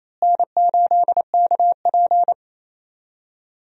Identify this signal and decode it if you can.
Morse code, heard as a keyed tone, about 28 wpm, 700 Hz. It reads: N8KP
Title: USA Call Signs - 4 Characters - Repeated 2x - 28wpm